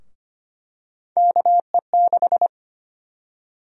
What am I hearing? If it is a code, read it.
Morse code, heard as a keyed tone, about 25 wpm, 700 Hz. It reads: KE6
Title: USA Call Sign Prefixes - Repeated 2x - 25wpm